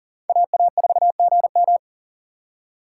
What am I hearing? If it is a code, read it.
Morse code, heard as a keyed tone, about 40 wpm, 700 Hz. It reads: AA4GM
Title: USA Call Signs - 5 Characters - 6x Character Spacing - 40wpm